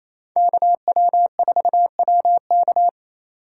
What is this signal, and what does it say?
Morse code, heard as a keyed tone, about 28 wpm, 700 Hz. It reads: KW4WK
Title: USA Call Signs - 5 Characters - 8x Character Spacing - 28wpm